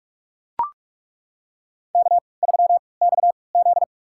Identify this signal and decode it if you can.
Morse code, heard as a keyed tone, about 41 wpm, spaced as for 25 wpm, 690 Hz. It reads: K3XZ
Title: USA Call Signs - 4 Characters - 3x Character Spacing - 45wpm